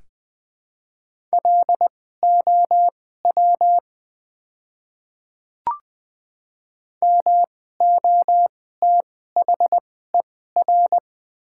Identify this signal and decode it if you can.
Morse code, heard as a keyed tone, about 20 wpm, spaced as for 14 wpm, 700 Hz. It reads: LOW MOTHER
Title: 2x Character Spacing - Top 200 Words - 20wpm